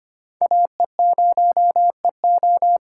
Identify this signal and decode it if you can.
Morse code, heard as a keyed tone, about 25 wpm, 700 Hz. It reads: AE0EO
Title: USA Call Signs - 5 Characters - 3x Character Spacing - 25wpm